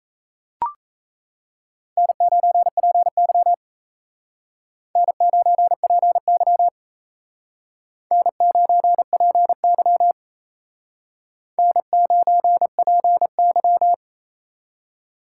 Morse code, heard as a keyed tone, about 35 wpm, 700 Hz. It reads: N9PY N9PY N9PY N9PY